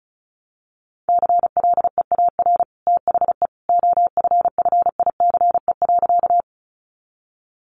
Morse code, heard as a keyed tone, about 34 wpm, 700 Hz. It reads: CLEAR THE OFFICE.
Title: Sentences from Top 1000 Words 35wpm